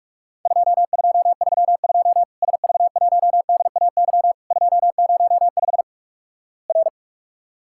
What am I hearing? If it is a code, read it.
Morse code, heard as a keyed tone, about 46 wpm, 700 Hz. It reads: 2232 SV1DAY 205 R
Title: Worked All Europe DX Contest - QTC Exchange - 45wpm